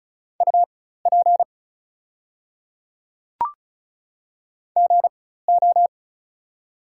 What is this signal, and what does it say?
Morse code, heard as a keyed tone, about 36 wpm, spaced as for 16 wpm, 700 Hz. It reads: UP GO
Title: 4x Character Spacing - Top 200 Words - 35wpm